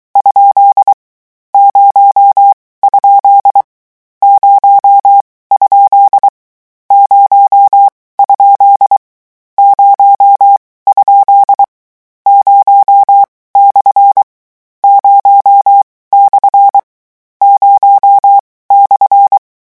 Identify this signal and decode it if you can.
Morse code, heard as a keyed tone, about 23 wpm, 800 Hz. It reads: ? 0 ? 0 ? 0 ? 0 ? 0 / 0 / 0 /
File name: morse mp3 / lecon13-vitesse 22